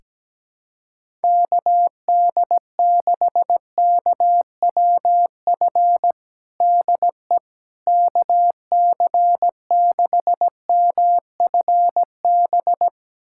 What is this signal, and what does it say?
Morse code, heard as a keyed tone, about 17 wpm, 700 Hz. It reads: KD6KWF DE KC6MFB
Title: USA Callsign DE Callsign - 6 Characters - 17wpm